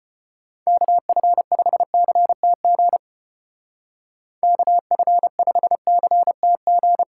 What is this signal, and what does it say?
Morse code, heard as a keyed tone, about 33 wpm, 700 Hz. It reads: KF5CTG KF5CTG